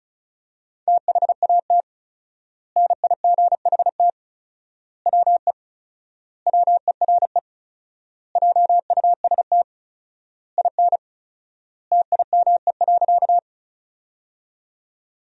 Morse code, heard as a keyed tone, about 36 wpm, 700 Hz. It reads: THAT NIGHT WE WERE JUST IN TIME.
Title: 4x Word-Spacing - Sentences from Top 300 Words - 35wpm